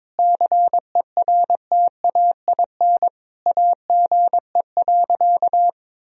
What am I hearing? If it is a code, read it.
Morse code, heard as a keyed tone, about 22 wpm, 700 Hz. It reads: CERTAIN AGE.
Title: Sentences from Top 1000 Words - Review Part 3 - 22wpm